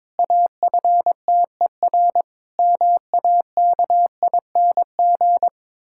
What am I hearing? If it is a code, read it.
Morse code, heard as a keyed tone, about 22 wpm, 700 Hz. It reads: AFTER MAKING
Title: Pangram Challenge v2 - 22wpm